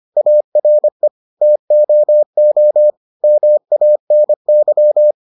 Morse code, heard as a keyed tone, about 25 wpm, 600 Hz. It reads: ARE TOO MANY